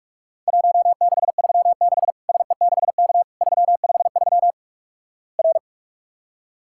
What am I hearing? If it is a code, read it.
Morse code, heard as a keyed tone, about 46 wpm, 700 Hz. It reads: 1636 SE6K 353 R
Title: Worked All Europe DX Contest - QTC Exchange - 45wpm